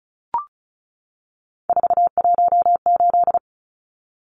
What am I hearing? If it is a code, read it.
Morse code, heard as a keyed tone, about 34 wpm, 700 Hz. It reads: 418
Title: Numbers - 3-Digit 35wpm